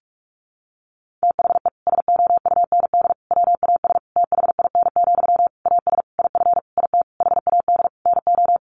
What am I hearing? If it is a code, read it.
Morse code, heard as a keyed tone, about 44 wpm, 700 Hz. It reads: THE SOUND WAS THIN, AS IF IT HAD NO
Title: Sentences from Top 1200 Words 45wpm